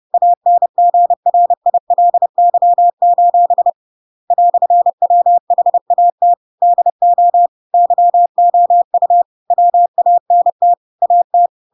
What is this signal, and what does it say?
Morse code, heard as a keyed tone, about 30 wpm, 700 Hz. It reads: ANGRILY: "WHAT DO YOU WANT AT
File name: Pinocchio / Pinocchio ch6 30 30 ik1hge